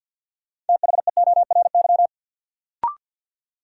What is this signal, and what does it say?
Morse code, heard as a keyed tone, about 52 wpm, 700 Hz. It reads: THEORY
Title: Sentences from Top 600 Words - New Word Drill - 50wpm